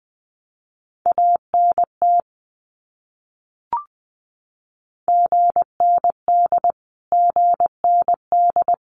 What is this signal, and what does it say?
Morse code, heard as a keyed tone, about 20 wpm, 700 Hz. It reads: ANT GND GND
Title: Abbreviations Repeated 3x - 20wpm